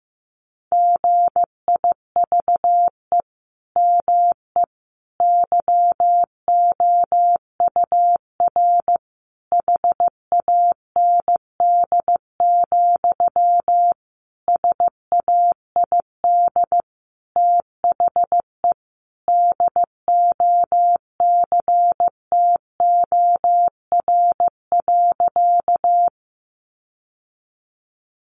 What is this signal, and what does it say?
Morse code, heard as a keyed tone, about 15 wpm, 700 Hz. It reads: GIVE ME YOUR HAND, SAID THE DOCTOR.
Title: Sentences from Top 1500 Words 15wpm